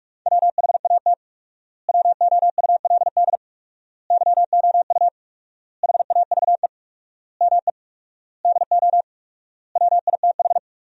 Morse code, heard as a keyed tone, about 42 wpm, 700 Hz. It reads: WHAT WOULD YOU HAVE ME DO WITH
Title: 4x Word-Spacing - Sentences from Top 300 Words - 45wpm